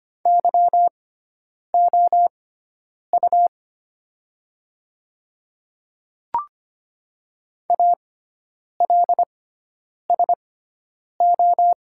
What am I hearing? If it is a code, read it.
Morse code, heard as a keyed tone, about 25 wpm, spaced as for 9 wpm, 700 Hz. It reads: YOU ALSO
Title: 6x Character Spacing - Top 500 Words - 25wpm